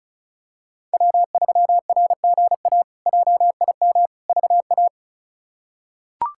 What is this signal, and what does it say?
Morse code, heard as a keyed tone, about 36 wpm, 700 Hz. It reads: W3RGA JIM VA
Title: K1USN SST QSO Practice - 35wpm